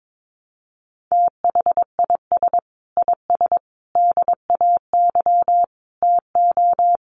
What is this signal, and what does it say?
Morse code, heard as a keyed tone, about 22 wpm, 700 Hz. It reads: THIS IS DAY TO
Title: Sets of 5 Words Encore 22wpm